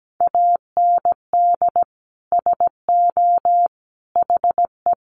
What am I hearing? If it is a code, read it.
Morse code, heard as a keyed tone, about 17 wpm, 700 Hz. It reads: AND SO HE
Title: Sentences from Top 500 Words 17wpm